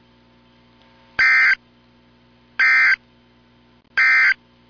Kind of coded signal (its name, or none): Emergency Alert System (SAME) tones